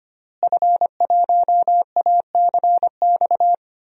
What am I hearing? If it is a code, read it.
Morse code, heard as a keyed tone, about 25 wpm, 700 Hz. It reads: F1ACX
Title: DXCC-100 Call Signs - 5 Characters - Repeated 3x - 25wpm